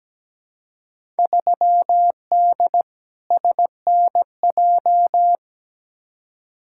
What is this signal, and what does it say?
Morse code, heard as a keyed tone, about 17 wpm, 700 Hz. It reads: 3D SNJ
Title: Field Day Exchanges Encore - 17wpm